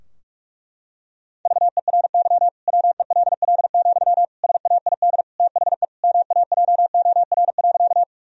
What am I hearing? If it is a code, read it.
Morse code, heard as a keyed tone, about 41 wpm, 700 Hz. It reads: VERY WELL, SAID THE MAJOR.
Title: Sentences from Top 1000 Words - Review Part 4 - 45wpm